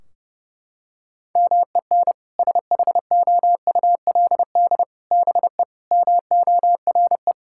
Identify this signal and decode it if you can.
Morse code, heard as a keyed tone, about 32 wpm, 700 Hz. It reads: MEN SHOULD BE MORE
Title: Sentences from Top 1600 Words 30wpm